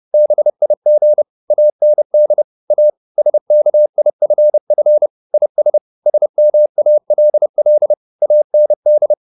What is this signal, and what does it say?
Morse code, heard as a keyed tone, about 30 wpm, 600 Hz. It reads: BIG AND A SKIFF IS SMALL AND